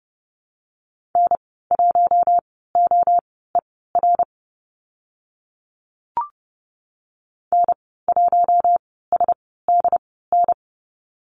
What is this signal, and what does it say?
Morse code, heard as a keyed tone, about 30 wpm, spaced as for 17 wpm, 700 Hz. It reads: N1OER N1SDN
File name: Lesson 14g - Number 2 - Call Signs 30wpm